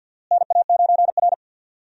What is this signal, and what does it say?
Morse code, heard as a keyed tone, about 52 wpm, 700 Hz. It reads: NA9R
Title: USA Call Signs - 4 Characters - Repeated 2x - 50wpm